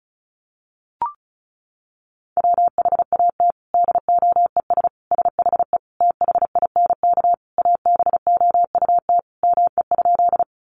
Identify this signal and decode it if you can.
Morse code, heard as a keyed tone, about 34 wpm, 700 Hz. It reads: WHAT DOES SHE THINK ABOUT ME?
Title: Sentences from Top 200 Words 35wpm